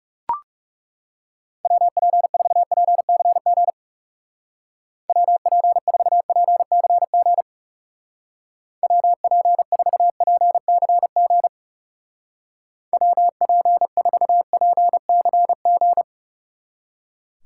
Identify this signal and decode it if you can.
Morse code, heard as a keyed tone, about 36 wpm, 700 Hz. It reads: WP4PCG WP4PCG WP4PCG WP4PCG
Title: Speed Racing - USA 2x3 Call Signs 30wpm